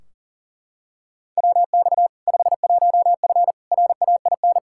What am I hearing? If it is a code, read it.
Morse code, heard as a keyed tone, about 40 wpm, 700 Hz. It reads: WX 51F RAIN
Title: K1USN SST QSO Practice 40wpm